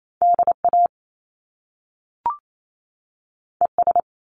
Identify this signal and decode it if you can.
Morse code, heard as a keyed tone, about 28 wpm, 700 Hz. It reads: DA ES
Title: Top Two Letter Combinations in Words 28wpm